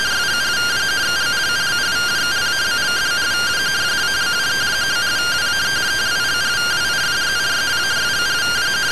Two-tone teleprinter signal, about 170 Hz shift.